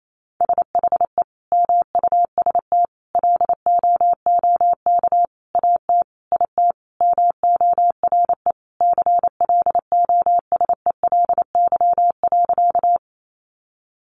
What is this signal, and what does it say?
Morse code, heard as a keyed tone, about 28 wpm, 700 Hz. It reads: SHE MUST LOOK AT IT MORE CLOSELY.
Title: Sentences from Top 1400 Words 28wpm